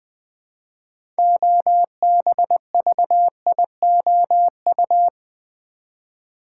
Morse code, heard as a keyed tone, about 20 wpm, 700 Hz. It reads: OBVIOU
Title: Build A Word - 3 Letters At A Time - 9-Letter Words - 20wpm